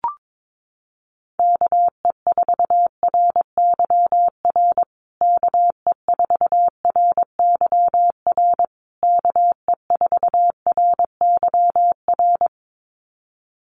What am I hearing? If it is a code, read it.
Morse code, heard as a keyed tone, about 22 wpm, 700 Hz. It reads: KE4RYR KE4RYR KE4RYR
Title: USA Call Signs Repeated 3x 22wpm